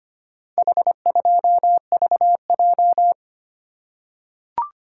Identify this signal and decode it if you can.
Morse code, heard as a keyed tone, about 25 wpm, 700 Hz. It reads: H2VJ
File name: practice / 25wpm / Four Letter-Number Combinations - 25wpm